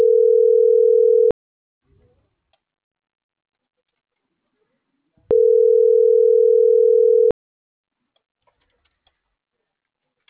Ringback tone (North America), 440 + 480 Hz.